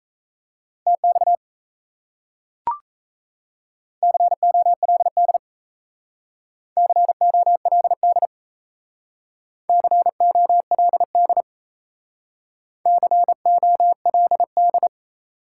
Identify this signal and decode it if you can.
Morse code, heard as a keyed tone, about 36 wpm, 700 Hz. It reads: TX COLD COLD COLD COLD